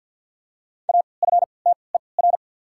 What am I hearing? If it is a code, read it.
Morse code, heard as a keyed tone, about 53 wpm, spaced as for 29 wpm, 690 Hz. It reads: AFTER
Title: 3x Character Spacing - Top 500 Words - 50wpm